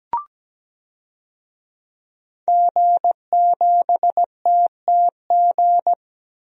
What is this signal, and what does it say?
Morse code, heard as a keyed tone, about 17 wpm, 700 Hz. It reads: G7TTG
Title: DXCC-100 Call Signs - 5 Characters - 17wpm